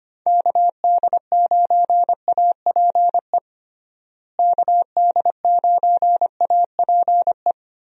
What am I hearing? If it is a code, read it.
Morse code, heard as a keyed tone, about 25 wpm, 700 Hz. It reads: KD9APE KD9APE
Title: USA Call Signs - 6 Characters - Repeated 2x - 25wpm